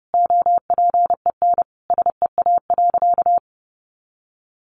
Morse code, heard as a keyed tone, about 30 wpm, 700 Hz. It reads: OPEN SEA.
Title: Sentences from Top 1500 Words 30wpm